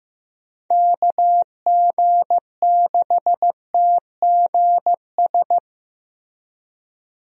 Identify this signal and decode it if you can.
Morse code, heard as a keyed tone, about 15 wpm, 700 Hz. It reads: KG6TGS
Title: USA Call Signs - 6 Characters - Repeated 2x - 15wpm